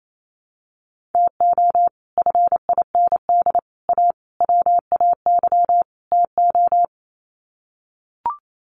Morse code, heard as a keyed tone, about 28 wpm, 700 Hz. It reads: TO FIND A WAY TO
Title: Sets of 5 Words 28wpm